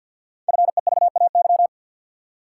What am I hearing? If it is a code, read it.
Morse code, heard as a keyed tone, about 48 wpm, 700 Hz. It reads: FEVAY
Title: Code Groups - 5 Letter:Numbers - 50wpm